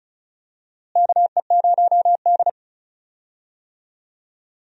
Morse code, heard as a keyed tone, about 36 wpm, 700 Hz. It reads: KE0D
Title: USA Call Signs - 4 Characters - Repeated 3x - 35wpm